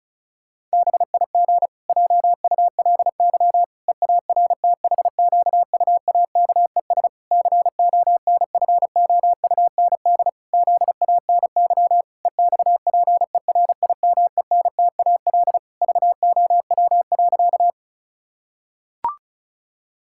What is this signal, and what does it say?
Morse code, heard as a keyed tone, about 36 wpm, 700 Hz. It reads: BIG JULY EARTHQUAKES CONFOUND ZANY EXPERIMENTAL VOW.
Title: Pangram Challenge v2 - 35wpm